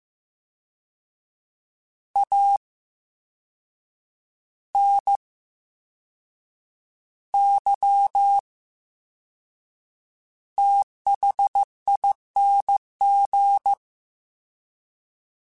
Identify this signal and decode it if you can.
Morse code, heard as a keyed tone, about 15 wpm, spaced as for 6 wpm, 800 Hz. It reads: ANYTHING
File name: Special INGs / Special INGs-mp3s / Slow ing @13